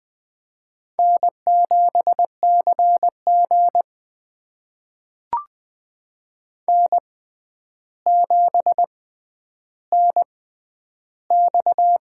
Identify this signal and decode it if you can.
Morse code, heard as a keyed tone, about 20 wpm, spaced as for 10 wpm, 700 Hz. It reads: N7CG N7NX
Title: USA Call Signs - 6x Character Spacing - 20wpm